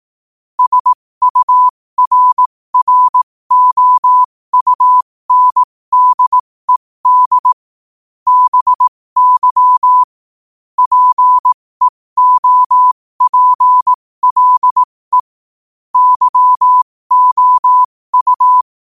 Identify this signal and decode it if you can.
Morse code, heard as a keyed tone, about 18 wpm, spaced as for 15 wpm, 1000 Hz. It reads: SURROUNDED BY PEOPLE YOU
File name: qotdfiles / 15wpm